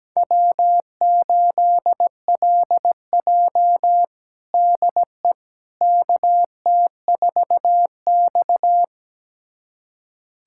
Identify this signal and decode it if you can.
Morse code, heard as a keyed tone, about 17 wpm, 700 Hz. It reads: W8LJ DE KT4X
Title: USA Callsign DE Callsign - 4 Characters - 17wpm